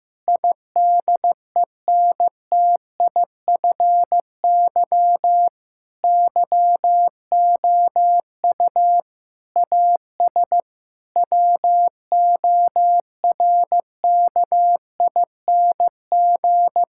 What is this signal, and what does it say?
Morse code, heard as a keyed tone, about 15 wpm, 700 Hz. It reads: IDENTIFY YOU AS WORKING
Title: Sentences from Top 1000 Words - Review Part 6 - 15wpm